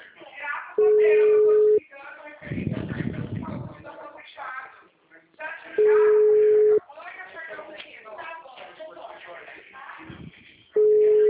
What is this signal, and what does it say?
Ringback tone (Europe, 425 Hz), with speech.
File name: out- -2010- - - .52274